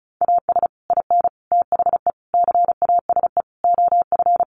Morse code, heard as a keyed tone, about 34 wpm, 700 Hz. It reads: AS IN THE CASE OF
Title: Sets of 5 Words 35wpm